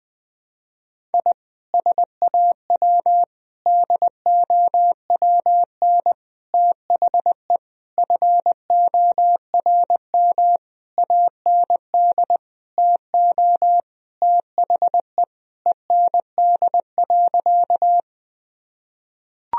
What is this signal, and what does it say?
Morse code, heard as a keyed tone, about 21 wpm, 700 Hz. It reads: I SAW DOWN THE FORM AND TO THE END.
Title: Sentences from Top 1000 Words - Review Part 1 - 20wpm